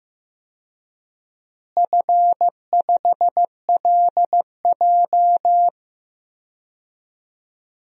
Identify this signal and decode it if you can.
Morse code, heard as a keyed tone, about 15 wpm, 700 Hz. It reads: F5LJ
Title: DXCC-100 Call Signs - 4 Characters - Repeated 2x - 15wpm